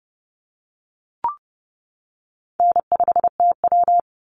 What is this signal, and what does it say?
Morse code, heard as a keyed tone, about 30 wpm, 700 Hz. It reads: N5TW
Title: Easy Call Signs - 30wpm